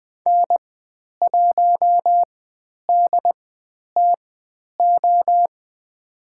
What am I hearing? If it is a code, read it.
Morse code, heard as a keyed tone, about 20 wpm, spaced as for 10 wpm, 700 Hz. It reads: N1DTO
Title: 095 Lesson 16g - Letter c - Call Signs 20wpm